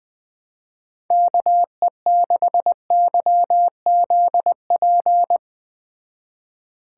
Morse code, heard as a keyed tone, about 19 wpm, 700 Hz. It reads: KE6YZP
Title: Speed Racing - USA 2x3 Call Signs 17wpm